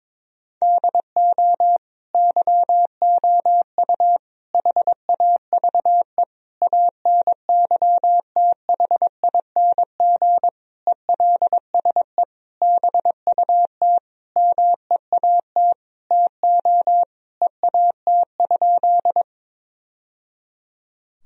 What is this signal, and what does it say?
Morse code, heard as a keyed tone, about 22 wpm, 700 Hz. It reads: DO YOU HAVE ANYTHING ELSE BUT MEAT TO EAT?
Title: Sentences from Top 1000 Words - Review Part 3 - 22wpm